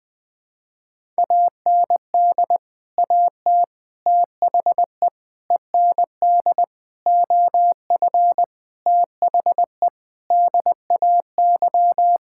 Morse code, heard as a keyed tone, about 20 wpm, 700 Hz. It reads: AND AT THE END OF THE DAY
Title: Sentences with 5-Word Phrases - 20wpm